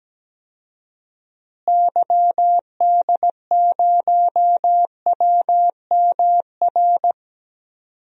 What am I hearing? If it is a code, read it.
Morse code, heard as a keyed tone, about 17 wpm, 700 Hz. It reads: YD0WMR
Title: DXCC-100 Call Signs - 6 Characters - Repeated 3x - 17wpm